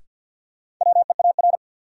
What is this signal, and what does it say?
Morse code, heard as a keyed tone, about 53 wpm, 700 Hz. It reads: WEAR
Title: Sentences from Top 1000 Words - Review Part 6 - New Word Drill - 50wpm